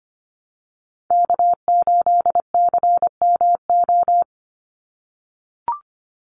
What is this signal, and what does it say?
Morse code, heard as a keyed tone, about 25 wpm, 700 Hz. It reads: K8CMO
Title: Call Signs - 25wpm